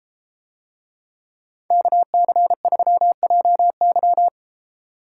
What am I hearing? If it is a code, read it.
Morse code, heard as a keyed tone, about 32 wpm, 700 Hz. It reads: KC3JY